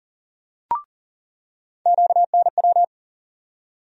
Morse code, heard as a keyed tone, about 40 wpm, 700 Hz. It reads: QNW
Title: USA Call Sign Postfixes - 3 Letters 40wpm